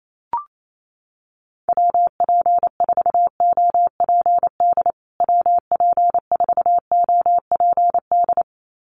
Morse code, heard as a keyed tone, about 28 wpm, 700 Hz. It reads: WP4OPD WP4OPD
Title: USA Call Signs Repeated 2x 28wpm